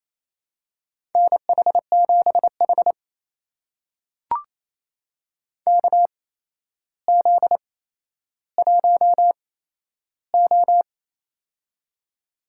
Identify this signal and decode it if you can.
Morse code, heard as a keyed tone, about 28 wpm, spaced as for 12 wpm, 700 Hz. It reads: NH7H KZ1O
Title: USA Call Signs - 4 Characters - 8x Character Spacing - 28wpm